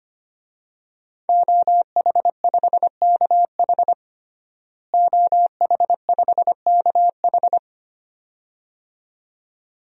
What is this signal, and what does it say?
Morse code, heard as a keyed tone, about 25 wpm, 700 Hz. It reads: OH5KH OH5KH
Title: DXCC-100 Call Signs - 5 Characters - Repeated 2x - 25wpm